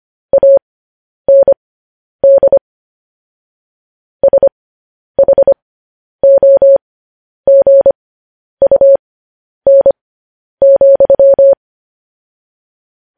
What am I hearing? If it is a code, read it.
Morse code, heard as a keyed tone, about 25 wpm, spaced as for 10 wpm, 550 Hz. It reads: AND SHOGUN,
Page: news-headlines-in-morse-code-at-10-wpm